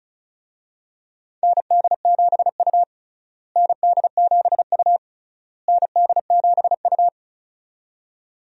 Morse code, heard as a keyed tone, about 36 wpm, 700 Hz. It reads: ND7U ND7U ND7U
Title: USA Call Signs - 4 Characters - Repeated 3x - 35wpm